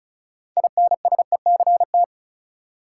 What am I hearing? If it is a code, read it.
Morse code, heard as a keyed tone, about 34 wpm, 700 Hz. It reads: INSECT
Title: Six Letter Words - 35wpm